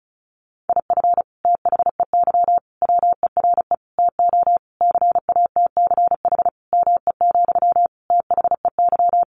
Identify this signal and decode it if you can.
Morse code, heard as a keyed tone, about 34 wpm, 700 Hz. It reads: IF THEY WERE TO CATCH ME, THEY
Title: Sentences from Top 1500 Words 35wpm